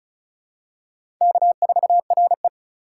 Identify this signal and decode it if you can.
Morse code, heard as a keyed tone, about 34 wpm, 700 Hz. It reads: K4RE
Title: USA Call Signs - 4 Characters - 3x Character Spacing - 35wpm